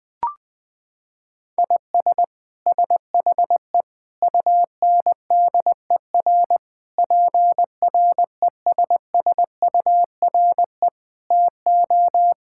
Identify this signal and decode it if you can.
Morse code, heard as a keyed tone, about 19 wpm, 700 Hz. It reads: IS SHE UNDER PRESSURE TO
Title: Sentences from Top 1000 Words - Review Part 6 - 20wpm